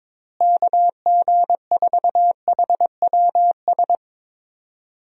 Morse code, heard as a keyed tone, about 22 wpm, 700 Hz. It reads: KG4HWS
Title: USA Call Signs - 6 Characters - Repeated 2x - 22wpm